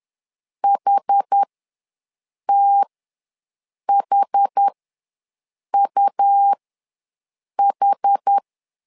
Morse code, heard as a keyed tone, about 10 wpm, spaced as for 6 wpm, 800 Hz. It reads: HTHUH
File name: LEZIONI / 2 UHDN / 2 UH ws 10wpm 2029